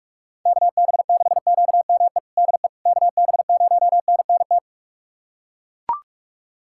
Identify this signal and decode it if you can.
Morse code, heard as a keyed tone, about 41 wpm, 700 Hz. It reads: KB6QME DE KB0NNT
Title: USA Callsign DE Callsign - 6 Characters - 45wpm